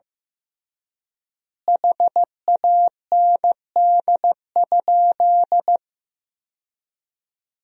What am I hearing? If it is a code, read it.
Morse code, heard as a keyed tone, about 15 wpm, 700 Hz. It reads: HAND?
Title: 3x Word-Spacing - Sentences from Top 200 Words - 15wpm